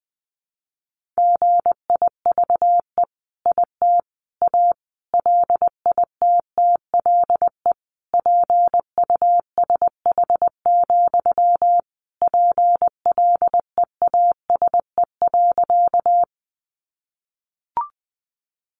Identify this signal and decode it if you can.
Morse code, heard as a keyed tone, about 20 wpm, 700 Hz. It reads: GIVE IT A LITTLE PUSH, PLEASE.
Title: Sentences from Top 1500 Words 20wpm